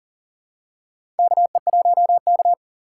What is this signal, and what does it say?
Morse code, heard as a keyed tone, about 38 wpm, 700 Hz. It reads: KE1K
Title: USA Call Signs - 4 Characters - 3x Character Spacing - 40wpm